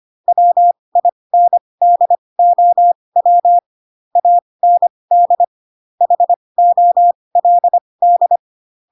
Morse code, heard as a keyed tone, about 25 wpm, spaced as for 20 wpm, 700 Hz. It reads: WINDOW AND HOLD
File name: Pinocchio / Pinocchio ch6 15 25 ik1hge